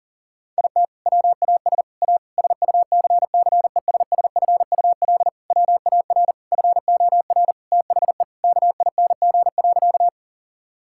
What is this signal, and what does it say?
Morse code, heard as a keyed tone, about 37 wpm, 700 Hz. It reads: IT WAS A SUCCESSFUL WAR FOR THE KING.
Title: Sentences from Top 1000 Words - Review Part 6 - 40wpm